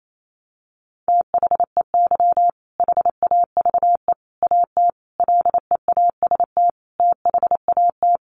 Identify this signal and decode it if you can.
Morse code, heard as a keyed tone, about 28 wpm, 700 Hz. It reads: THEY HAVE AT LEAST THAT
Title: Sentences from Top 900 Words 28wpm